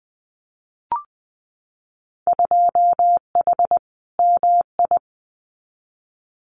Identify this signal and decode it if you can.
Morse code, heard as a keyed tone, about 20 wpm, 700 Hz. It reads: 2H MI
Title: Winter Field Day Exchanges 20wpm